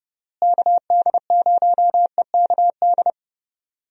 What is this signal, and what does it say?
Morse code, heard as a keyed tone, about 30 wpm, 700 Hz. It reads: KD0EKD
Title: USA Call Signs - 6 Characters - Repeated 3x - 30wpm